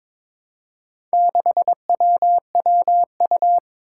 Morse code, heard as a keyed tone, about 22 wpm, 700 Hz. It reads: 6WWU
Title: Four Challenging Letter-Number Combinations 22wpm